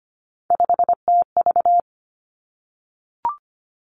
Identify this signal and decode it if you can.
Morse code, heard as a keyed tone, about 25 wpm, 700 Hz. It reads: 5TV
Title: Three Challenging Letter-Number Combinations 25wpm